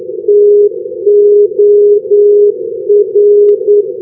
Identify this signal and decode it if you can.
Morse code, heard as a keyed tone, about 9 wpm, 410 Hz. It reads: TOR
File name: europe / fin / fin-299 ndb rovaniemi-toramo tor-416 np-s-haparanda-2012-k3-rb